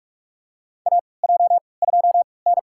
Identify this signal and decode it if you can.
Morse code, heard as a keyed tone, about 46 wpm, spaced as for 26 wpm, 700 Hz. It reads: AJ2N
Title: USA Call Signs - 4 Characters - 3x Character Spacing - 45wpm